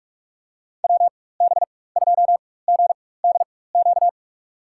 Morse code, heard as a keyed tone, about 43 wpm, spaced as for 21 wpm, 700 Hz. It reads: WB2GDQ
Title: USA Call Signs - 6 Characters - 4x Character Spacing - 45wpm